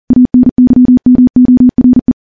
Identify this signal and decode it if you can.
Morse code, heard as a keyed tone, about 40 wpm, 250 Hz. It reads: ANYMORE
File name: morse / morse-training-40WPM